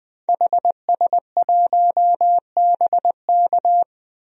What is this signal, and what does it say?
Morse code, heard as a keyed tone, about 21 wpm, 700 Hz. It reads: HS1BK
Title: DXCC-100 Call Signs - 5 Characters - 20wpm